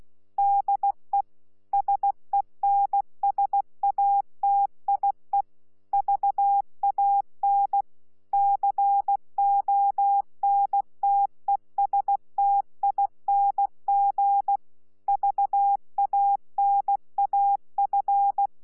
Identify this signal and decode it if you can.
Morse code, heard as a keyed tone, about 16 wpm, 800 Hz. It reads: DE SENSATIE VAN CONTESTING VANAF
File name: Z PI4VRZ / PI4VRZ 14-jun-2025